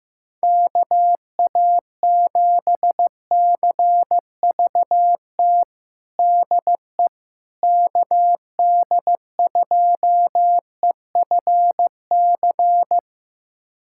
Morse code, heard as a keyed tone, about 15 wpm, 700 Hz. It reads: KA7CVT DE KD2EFC
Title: USA Callsign DE Callsign 15wpm